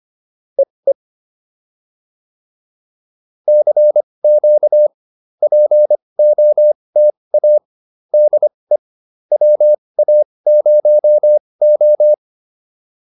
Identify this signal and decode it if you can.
Morse code, heard as a keyed tone, about 25 wpm, spaced as for 20 wpm, 600 Hz. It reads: EE CQ POTA DE WA0O